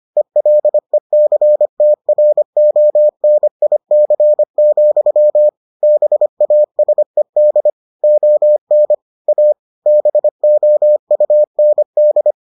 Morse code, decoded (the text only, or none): ELECTRONIC, BASED ON A BOUND